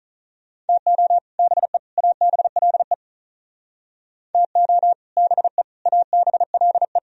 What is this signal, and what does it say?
Morse code, heard as a keyed tone, about 45 wpm, 700 Hz. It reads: TO BE ABLE TO BE ABLE